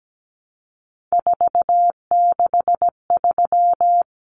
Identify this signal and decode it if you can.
Morse code, heard as a keyed tone, about 17 wpm, 700 Hz. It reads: 463
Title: Common Cut Numbers - 3-Digit 17wpm